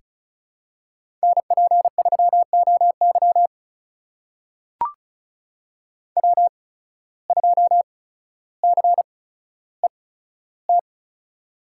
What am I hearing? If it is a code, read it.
Morse code, heard as a keyed tone, about 36 wpm, spaced as for 15 wpm, 700 Hz. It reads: NP3OY W2CET
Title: USA Call Signs - 8x Character Spacing - 35wpm